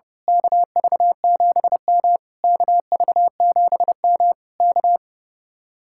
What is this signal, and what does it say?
Morse code, heard as a keyed tone, about 29 wpm, 700 Hz. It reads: KV7M KV7M K
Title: Speed-Racing Common QSO Phrases - 30wpm